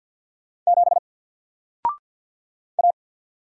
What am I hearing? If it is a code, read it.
Morse code, heard as a keyed tone, about 50 wpm, 700 Hz. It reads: 7 A
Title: Single Letter-Number 50wpm